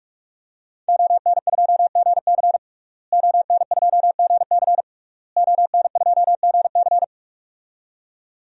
Morse code, heard as a keyed tone, about 46 wpm, 700 Hz. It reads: ON2GC ON2GC ON2GC
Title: DXCC-100 Call Signs - 5 Characters - Repeated 3x - 45wpm